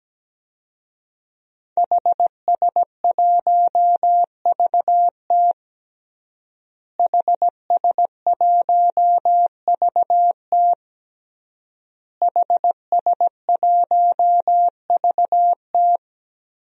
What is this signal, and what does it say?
Morse code, heard as a keyed tone, about 17 wpm, 700 Hz. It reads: HS1VT HS1VT HS1VT
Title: DXCC-100 Call Signs - 5 Characters - Repeated 3x - 17wpm